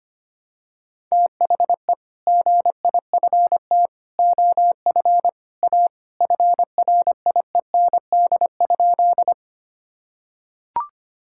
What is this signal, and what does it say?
Morse code, heard as a keyed tone, about 25 wpm, 700 Hz. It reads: THE GIFT OF A FRIEND?
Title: Sentences from Top 1000 Words - Review Part 6 - 25wpm